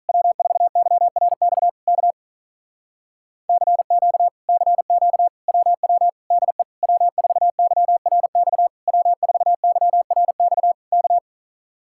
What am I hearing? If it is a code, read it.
Morse code, heard as a keyed tone, about 48 wpm, 700 Hz. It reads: W4YRX K CQ CQ WW DE W4YRX W4YRX K